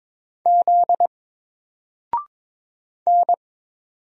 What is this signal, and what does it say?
Morse code, heard as a keyed tone, about 22 wpm, 700 Hz. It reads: Z N
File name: Single-Letter-Number-22wpm